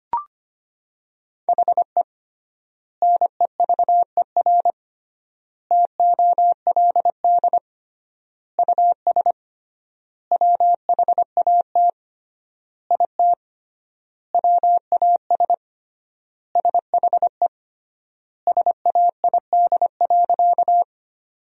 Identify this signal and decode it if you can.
Morse code, heard as a keyed tone, about 25 wpm, 700 Hz. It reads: HE NEVER TOLD US WHAT IT WAS SHE SAID.
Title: 3x Word-Spacing - Sentences from Top 400 Words - 25wpm